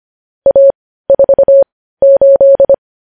Morse code, heard as a keyed tone, about 25 wpm, spaced as for 15 wpm, 550 Hz. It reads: A48
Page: news headlines in morse code at 15 wpm